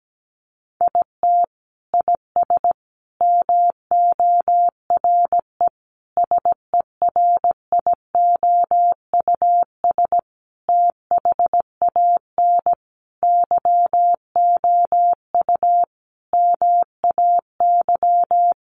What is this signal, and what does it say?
Morse code, heard as a keyed tone, about 17 wpm, 700 Hz. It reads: IT IS MORE SERIOUS THAN YOU MAY
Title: Sentences from Top 1100 Words 17wpm